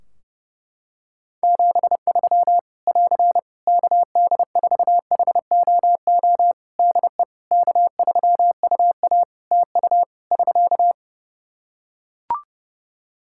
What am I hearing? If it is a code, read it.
Morse code, heard as a keyed tone, about 30 wpm, 700 Hz. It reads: 73 <AR> KD4HOO DE K3UA TU <SK>
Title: Practice QSO - Exchange 4 - Ending - 30wpm